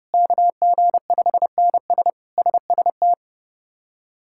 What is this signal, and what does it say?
Morse code, heard as a keyed tone, about 32 wpm, 700 Hz. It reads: KG5NS SST
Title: K1USN SST QSO Practice - 30wpm